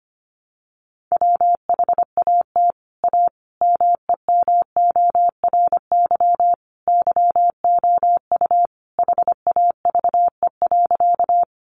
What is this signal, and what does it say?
Morse code, heard as a keyed tone, about 25 wpm, 700 Hz. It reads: WHAT A MEMORY YOU HAVE.
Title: Sentences from Top 1300 Words 25wpm